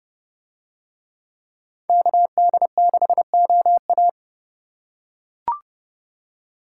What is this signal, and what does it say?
Morse code, heard as a keyed tone, about 30 wpm, 700 Hz. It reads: KD6OA